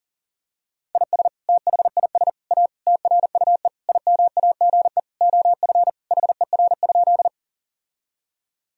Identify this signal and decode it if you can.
Morse code, heard as a keyed tone, about 40 wpm, 700 Hz. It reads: IS THIS A TRUE IMAGE OF HER?
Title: Sentences from Top 1000 Words - Review Part 4 - 40wpm